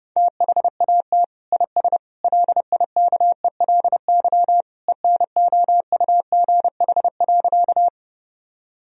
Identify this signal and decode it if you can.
Morse code, heard as a keyed tone, about 30 wpm, 700 Hz. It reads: THAT IS LIKELY ENOUGH.
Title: Sentences from Top 1000 Words - Review Part 4 - 30wpm